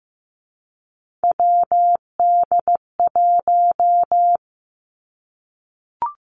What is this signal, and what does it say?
Morse code, heard as a keyed tone, about 15 wpm, 700 Hz. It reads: WD1
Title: Three Challenging Letter-Number Combinations 15wpm